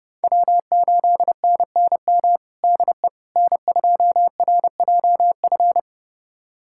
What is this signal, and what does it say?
Morse code, heard as a keyed tone, about 30 wpm, 700 Hz. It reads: W8NNM DE N2RJF
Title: USA Callsign DE Callsign - 5 Characters - 30wpm